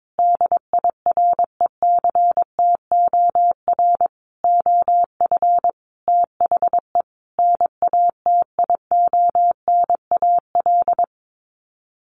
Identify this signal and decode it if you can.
Morse code, heard as a keyed tone, about 22 wpm, 700 Hz. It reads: DIRECTOR OF THE NATIONAL
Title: Sets of 5 Words Encore2 22wpm